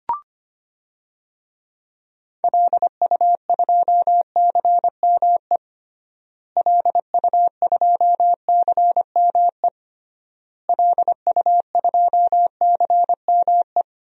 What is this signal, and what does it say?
Morse code, heard as a keyed tone, about 25 wpm, 700 Hz. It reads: LU2CME LU2CME LU2CME
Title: DXCC-100 Call Signs - 6 Characters - Repeated 3x - 25wpm